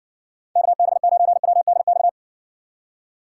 Morse code, heard as a keyed tone, about 60 wpm, 700 Hz. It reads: KB9WDX